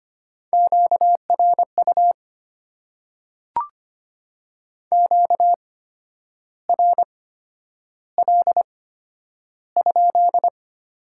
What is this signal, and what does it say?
Morse code, heard as a keyed tone, about 25 wpm, spaced as for 10 wpm, 700 Hz. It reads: QRU QRL?
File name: Top 90 QSO Elements - 8x Character Spacing - 25wpm